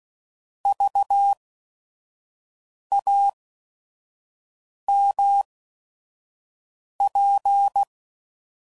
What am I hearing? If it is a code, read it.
Morse code, heard as a keyed tone, about 16 wpm, spaced as for 5 wpm, 800 Hz. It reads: VAMP
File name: Prefixes / By First Letter / Vs